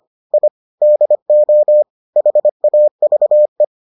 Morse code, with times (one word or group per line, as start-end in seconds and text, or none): 0.3-0.5 I
0.8-1.8 DO
2.2-3.6 HAVE